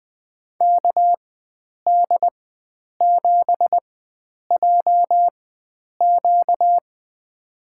Morse code, heard as a keyed tone, about 20 wpm, spaced as for 9 wpm, 700 Hz. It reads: KD7JQ
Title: USA Call Signs - 5 Characters - 4x Character Spacing - 20wpm